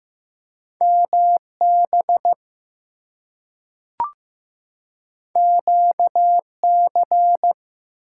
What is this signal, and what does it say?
Morse code, heard as a keyed tone, about 15 wpm, 700 Hz. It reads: MB QC
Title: US States and Canadian Provinces 15wpm